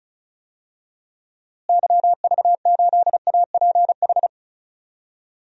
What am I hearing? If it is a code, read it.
Morse code, heard as a keyed tone, about 34 wpm, 700 Hz. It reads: YV8APH
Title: DXCC-100 Call Signs - 6 Characters - Repeated 3x - 35wpm